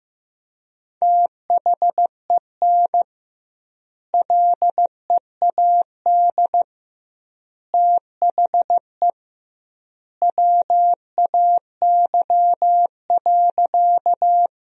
Morse code, heard as a keyed tone, about 15 wpm, 700 Hz. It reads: THEN LEAD THE WAY.
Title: 2x Word-Spacing - Sentences from Top 500 Words - 15wpm